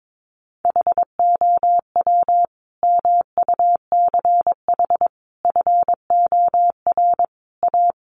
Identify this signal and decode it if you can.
Morse code, heard as a keyed tone, about 22 wpm, 700 Hz. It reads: HOW MUCH FOR A
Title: Sentences from Top 1400 Words 22wpm